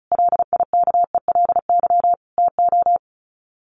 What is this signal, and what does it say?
Morse code, heard as a keyed tone, about 34 wpm, 700 Hz. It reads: LIKELY TO
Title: Sets of 4 Words 35wpm